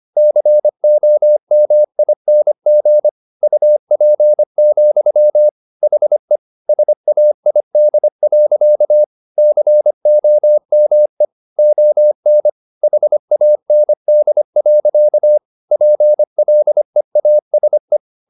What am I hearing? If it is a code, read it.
Morse code, heard as a keyed tone, about 25 wpm, 600 Hz. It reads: COMING UP, HE SAID. COME ON HAND. PLEASE
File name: the old man and the sea part 1 25wpm